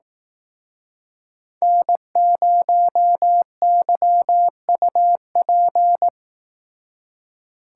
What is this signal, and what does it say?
Morse code, heard as a keyed tone, about 18 wpm, 700 Hz. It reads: N0YUP
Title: Speed Racing - USA 1x3 Call Signs 15wpm